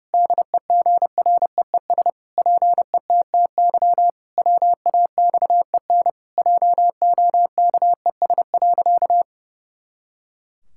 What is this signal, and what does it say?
Morse code, heard as a keyed tone, about 30 wpm, 700 Hz. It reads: DEGREES PETTY WAXEN JOKES.
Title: Pangram Challenge v2 - 30wpm